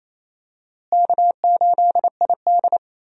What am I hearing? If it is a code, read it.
Morse code, heard as a keyed tone, about 28 wpm, 700 Hz. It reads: K8ID
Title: USA Call Signs - 4 Characters - Repeated 2x - 28wpm